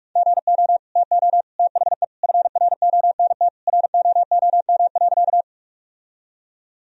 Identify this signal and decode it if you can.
Morse code, heard as a keyed tone, about 46 wpm, 700 Hz. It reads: GO TO THE FRONT ROOM.
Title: Sentences from Top 1000 Words - Review Part 2 - 45wpm